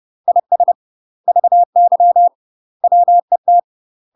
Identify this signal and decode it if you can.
Morse code, heard as a keyed tone, about 30 wpm, 700 Hz. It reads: IS VY WET
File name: UK MASTER QSOs TRAINING/30WPM